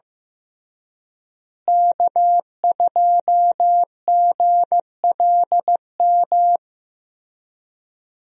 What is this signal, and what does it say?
Morse code, heard as a keyed tone, about 15 wpm, 700 Hz. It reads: K2GLM
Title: USA Call Signs - 5 Characters - Repeated 3x - 15wpm